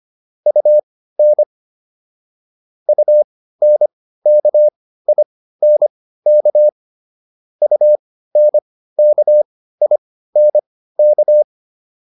Morse code, heard as a keyed tone, about 25 wpm, spaced as for 15 wpm, 600 Hz. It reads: UN UNKINK UNKINK